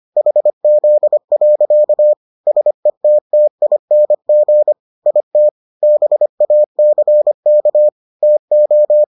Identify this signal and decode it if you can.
Morse code, heard as a keyed tone, about 25 wpm, 600 Hz. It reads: HZ. SETTING IT BACK TO